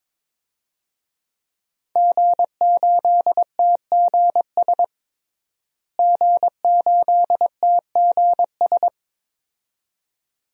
Morse code, heard as a keyed tone, about 22 wpm, 700 Hz. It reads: G8TGS G8TGS
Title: DXCC-100 Call Signs - 5 Characters - Repeated 2x - 22wpm